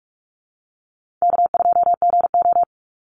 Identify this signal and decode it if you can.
Morse code, heard as a keyed tone, about 44 wpm, 700 Hz. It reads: K2GO
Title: Call Signs - 45wpm